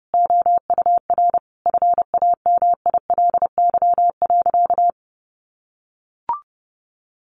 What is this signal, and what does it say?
Morse code, heard as a keyed tone, about 30 wpm, 700 Hz. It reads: OUR FAMILY.
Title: Sentences from Top 1300 Words 30wpm